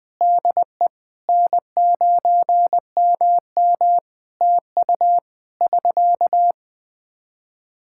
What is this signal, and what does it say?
Morse code, heard as a keyed tone, about 21 wpm, 700 Hz. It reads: DE N9MM TU <SK>
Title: Practice QSO - Exchange 4 - Ending - 20wpm